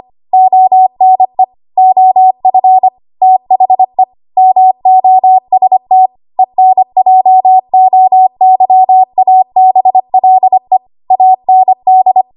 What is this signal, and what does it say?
Morse code, heard as a keyed tone, about 25 wpm, 750 Hz. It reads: ONE OF THE MOST ENJOYABLE AND
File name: Archive / 25 WPM / 25WPM